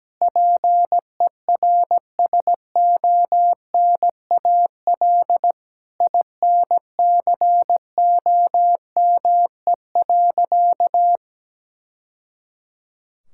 Morse code, heard as a keyed tone, about 17 wpm, 700 Hz. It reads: PERSONAL INCOME.
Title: Sentences from Top 1000 Words - Review Part 5 - 17wpm